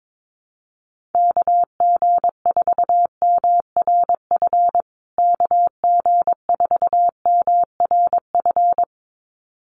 Morse code, heard as a keyed tone, about 22 wpm, 700 Hz. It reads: KG4MRF KG4MRF
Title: USA Call Signs Repeated 2x 22wpm